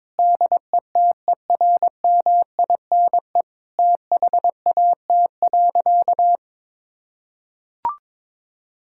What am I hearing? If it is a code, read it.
Morse code, heard as a keyed tone, about 22 wpm, 700 Hz. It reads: DETERMINE THAT.
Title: Sentences from Top 1000 Words - Review Part 5 - 22wpm